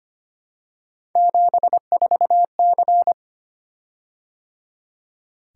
Morse code, heard as a keyed tone, about 25 wpm, 700 Hz. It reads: 74C
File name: Three Letter-Number Combinations - 25wpm